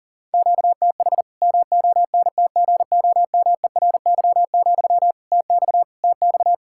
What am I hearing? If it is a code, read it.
Morse code, heard as a keyed tone, about 40 wpm, 700 Hz. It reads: QTH MONTGOMERY, TX TX
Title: K1USN SST QSO Practice 40wpm